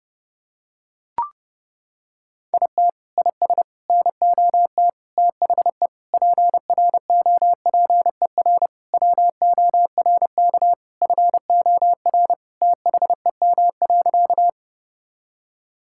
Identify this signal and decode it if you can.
Morse code, heard as a keyed tone, about 30 wpm, 700 Hz. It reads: IT IS NOT THE PROPER WORK FOR THEM.
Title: Sentences from Top 1000 Words - Review Part 5 - 30wpm